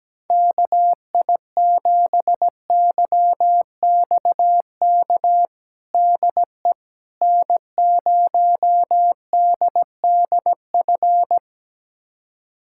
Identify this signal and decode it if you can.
Morse code, heard as a keyed tone, about 17 wpm, 700 Hz. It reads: KI7YXK DE N0DDF
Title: USA Callsign DE Callsign 17wpm